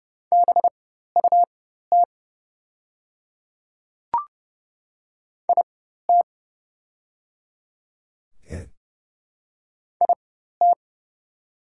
Morse code, heard as a keyed tone, about 30 wpm, spaced as for 14 wpm, 700 Hz. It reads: BUT IT IT
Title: 4x Character Spacing - Top 100 Words - 30wpm